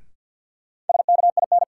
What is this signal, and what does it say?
Morse code, heard as a keyed tone, about 47 wpm, 700 Hz. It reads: SKIN
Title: Sentences from Top 1300 Words - New Word Drill Alt - 50wpm